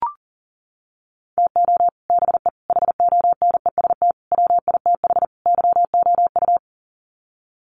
Morse code, heard as a keyed tone, about 40 wpm, 700 Hz. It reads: TO BE HONEST WITH YOU
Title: Sets of 5 Words Encore 40wpm